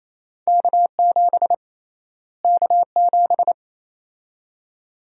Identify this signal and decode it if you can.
Morse code, heard as a keyed tone, about 28 wpm, 700 Hz. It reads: K7 K7
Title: USA Call Sign Prefixes - Repeated 2x - 28wpm